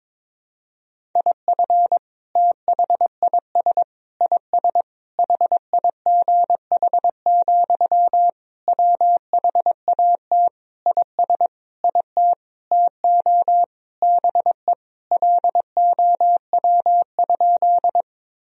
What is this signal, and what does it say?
Morse code, heard as a keyed tone, about 22 wpm, 700 Hz. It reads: IF THIS IS HIGH, WHAT IS IT TO BE LOW?
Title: Sentences from Top 1000 Words - Review Part 1 - 22wpm